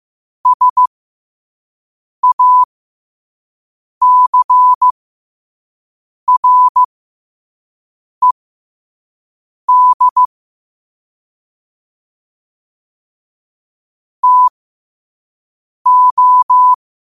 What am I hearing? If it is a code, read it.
Morse code, heard as a keyed tone, about 15 wpm, spaced as for 5 wpm, 1000 Hz. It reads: SACRED TO
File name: qotdfiles / 5wpm